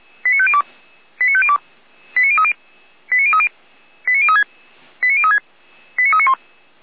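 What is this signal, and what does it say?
Five-tone selective call (ZVEI), heard as quick runs of five tones, tones of about 70 ms. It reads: 89502 89502 89030 89030 89036 89036 89391
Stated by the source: Five-tone ZVEI selective call received by SDR.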